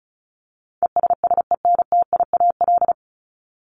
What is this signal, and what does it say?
Morse code, heard as a keyed tone, about 34 wpm, 700 Hz. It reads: ESSENTIAL
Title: LY Suffix - Top 100 Words with Base Words 35wpm